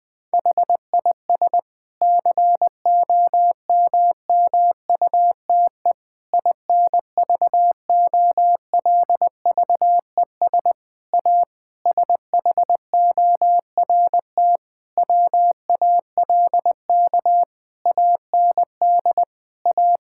Morse code, heard as a keyed tone, about 21 wpm, 700 Hz. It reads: HIS COMMUTE INVOLVES A SHORT WALK AND A
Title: Difficult Sentences with Hints - 20wpm